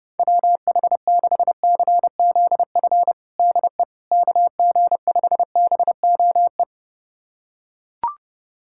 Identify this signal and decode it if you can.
Morse code, heard as a keyed tone, about 30 wpm, 700 Hz. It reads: WH6CZF DE KG5BOE
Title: USA Callsign DE Callsign - 6 Characters - 30wpm